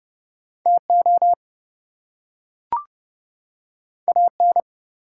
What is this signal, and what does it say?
Morse code, heard as a keyed tone, about 30 wpm, 700 Hz. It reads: TO AN
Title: Two Letter Words - 30wpm